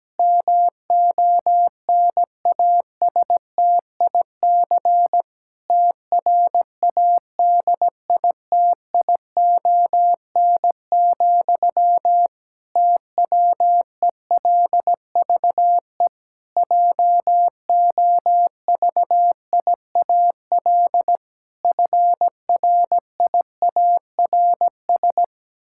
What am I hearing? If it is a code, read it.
Morse code, heard as a keyed tone, about 17 wpm, 700 Hz. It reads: MONASTIC TRADITION, TWELVE JOVIAL FRIARS
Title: Pangram Challenge v2 - 17wpm